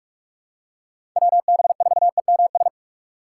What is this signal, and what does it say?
Morse code, heard as a keyed tone, about 47 wpm, 700 Hz. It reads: WB4EMS
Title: USA Call Signs - 3x Character Spacing - 45wpm